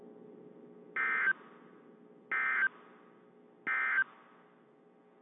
Emergency Alert System (SAME) tones, heard as harsh buzzing bursts. It is heard from 0.9 to 4.0 s.
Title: WZ2523 - NOAA Weather Radio